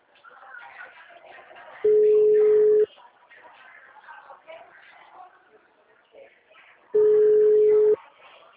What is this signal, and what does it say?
Ringback tone (Europe, 425 Hz), with speech.